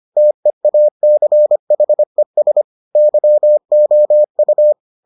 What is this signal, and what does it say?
Morse code, heard as a keyed tone, about 25 wpm, 600 Hz. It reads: TEACHES YOU